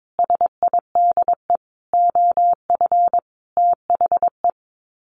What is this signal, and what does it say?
Morse code, heard as a keyed tone, about 22 wpm, 700 Hz. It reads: SIDE OF THE
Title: Sets of 5 Words Encore 22wpm